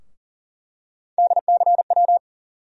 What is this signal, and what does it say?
Morse code, heard as a keyed tone, about 40 wpm, 700 Hz. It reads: DCW
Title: USA Call Sign Postfixes - 3 Letters 40wpm